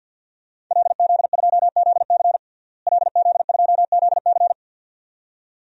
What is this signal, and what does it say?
Morse code, heard as a keyed tone, about 48 wpm, 700 Hz. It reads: LZ2ZC LZ2ZC
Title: DXCC-100 Call Signs - 5 Characters - Repeated 2x - 50wpm